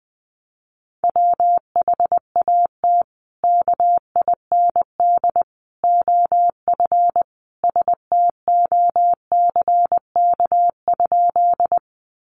Morse code, heard as a keyed tone, about 19 wpm, 700 Hz. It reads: WHAT KIND OF STOCK?
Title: Sentences from Top 800 Words 20wpm